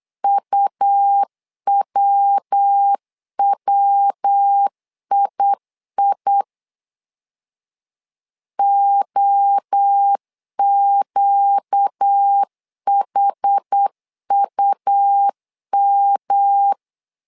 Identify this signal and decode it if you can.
Morse code, heard as a keyed tone, about 8 wpm, 800 Hz. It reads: UWWII OQHUM
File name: LEZIONI / 4 Q 8wpm 2732